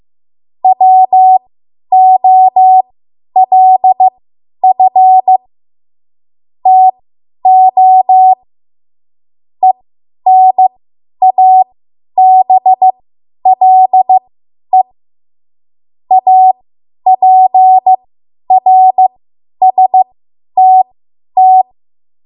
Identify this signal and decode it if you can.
Morse code, heard as a keyed tone, about 15 wpm, spaced as for 10 wpm, 750 Hz. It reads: WOLF TO ENABLE APRSTT
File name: Morse / 10WPM